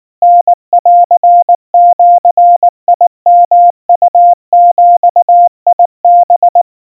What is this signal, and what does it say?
Morse code (Japanese Wabun code), heard as a keyed tone, about 19 wpm, 700 Hz. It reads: タンジヨウビハ
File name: kawa / Morse sample / kamyu 70 cpm 700 Hz Crispy